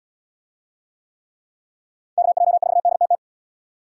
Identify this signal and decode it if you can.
Morse code, heard as a keyed tone, about 71 wpm, 700 Hz. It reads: KJ4DTT